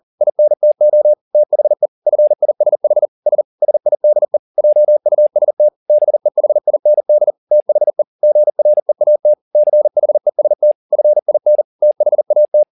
Morse code, heard as a keyed tone, about 40 wpm, 600 Hz. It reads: INTO THE FISH S SIDE JUST BEHIND THE GREAT CHEST FIN THAT
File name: the old man and the sea 2 40wpm